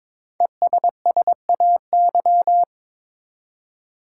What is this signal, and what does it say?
Morse code, heard as a keyed tone, about 22 wpm, 700 Hz. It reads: ESSAY
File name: 5 Letter Words - First 3 Letters Given - 22wpm